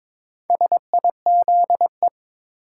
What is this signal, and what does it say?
Morse code, heard as a keyed tone, about 21 wpm, 700 Hz. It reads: SIZE
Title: Sets of 3 Words - New Word Drill - 22wpm